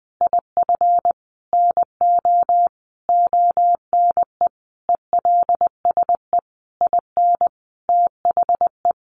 Morse code, heard as a keyed tone, about 20 wpm, 700 Hz. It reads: IF NO ONE ELSE IN THE
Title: Sentences from Top 1100 Words 20wpm